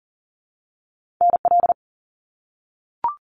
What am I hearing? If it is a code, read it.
Morse code, heard as a keyed tone, about 40 wpm, 700 Hz. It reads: NL
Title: Canadian Provinces and Territories - Repeated 3x - 40wpm